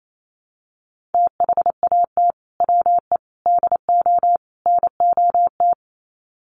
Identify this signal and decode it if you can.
Morse code, heard as a keyed tone, about 28 wpm, 700 Hz. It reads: THAT WE DO NOT
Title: Sets of 4 Words 28wpm